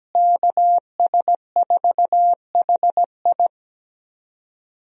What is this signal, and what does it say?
Morse code, heard as a keyed tone, about 17 wpm, 700 Hz. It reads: KS4HI
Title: USA Call Signs - 5 Characters - Repeated 3x - 17wpm